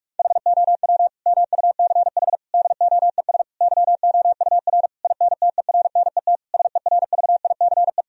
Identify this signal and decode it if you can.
Morse code, heard as a keyed tone, about 41 wpm, 700 Hz. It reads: HOW MUCH DOES YOUR INTERNET SERVICE
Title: Sentences from Top 1000 Words - Review Part 4 - 45wpm